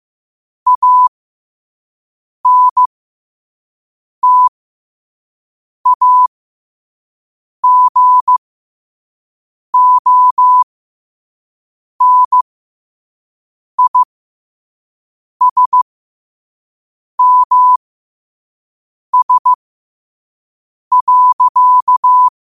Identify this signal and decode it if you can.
Morse code, heard as a keyed tone, about 15 wpm, spaced as for 5 wpm, 1000 Hz. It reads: ANTAGONISMS.
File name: qotdfiles / 5wpm